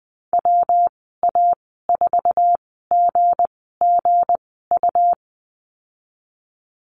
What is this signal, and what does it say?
Morse code, heard as a keyed tone, about 20 wpm, spaced as for 15 wpm, 700 Hz. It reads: WA4GGU
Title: 173 Lesson 28g - Number 9 - Call Signs 20wpm